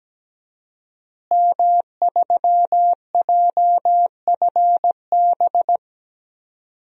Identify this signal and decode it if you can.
Morse code, heard as a keyed tone, about 17 wpm, 700 Hz. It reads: M3JFB
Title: DXCC-100 Call Signs - 5 Characters - Repeated 2x - 17wpm